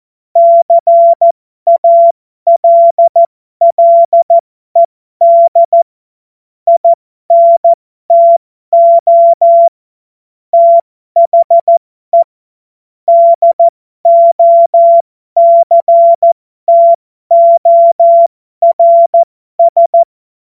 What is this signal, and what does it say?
Morse code, heard as a keyed tone, about 14 wpm, spaced as for 12 wpm, 680 Hz. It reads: CALLED INTO THE DOCTORS
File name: Morsekurs / Witze